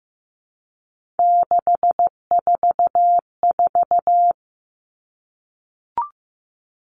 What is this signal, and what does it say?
Morse code, heard as a keyed tone, about 15 wpm, 700 Hz. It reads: 644
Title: Numbers - 3-Digit 15wpm